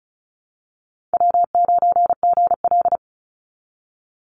Morse code, heard as a keyed tone, about 34 wpm, 700 Hz. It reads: W9GL
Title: Call Signs Repeated 2x - 35wpm